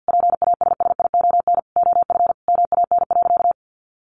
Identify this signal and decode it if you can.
Morse code, heard as a keyed tone, about 49 wpm, 700 Hz. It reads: PASSION OF MAN.
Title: Sentences from Top 1400 Words 50wpm